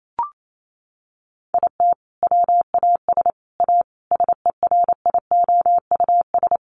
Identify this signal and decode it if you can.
Morse code, heard as a keyed tone, about 28 wpm, 700 Hz. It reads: IT WAS A SERIOUS
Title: Sentences from Top 1100 Words 28wpm